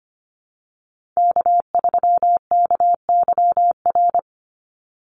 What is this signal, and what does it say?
Morse code, heard as a keyed tone, about 25 wpm, 700 Hz. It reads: K3KYR
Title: NA SOTA Chaser Callsigns 25wpm